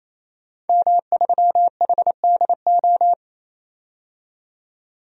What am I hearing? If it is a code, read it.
Morse code, heard as a keyed tone, about 28 wpm, 700 Hz. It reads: M3HDO
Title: DXCC-100 Call Signs - 5 Characters - 28wpm